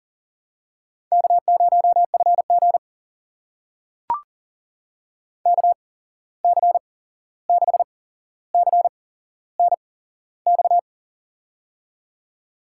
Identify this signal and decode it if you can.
Morse code, heard as a keyed tone, about 38 wpm, spaced as for 15 wpm, 700 Hz. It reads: K0FG KC6CNX
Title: USA Call Signs - 8x Character Spacing - 40wpm